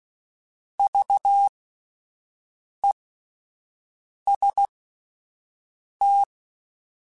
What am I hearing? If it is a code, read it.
Morse code, heard as a keyed tone, about 16 wpm, spaced as for 5 wpm, 800 Hz. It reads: VEST